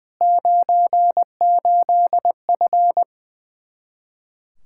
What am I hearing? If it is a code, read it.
Morse code, heard as a keyed tone, about 20 wpm, 700 Hz. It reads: 98F
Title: K1USN SST QSO Practice 20wpm